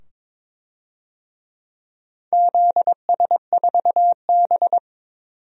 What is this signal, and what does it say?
Morse code, heard as a keyed tone, about 22 wpm, 700 Hz. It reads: ZS4B
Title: DXCC-100 Call Signs - 4 Characters - 22wpm